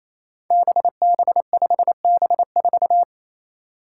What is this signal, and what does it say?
Morse code, heard as a keyed tone, about 27 wpm, 700 Hz. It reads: BB5B4
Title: Five Challenging Letter-Number Combinations 28wpm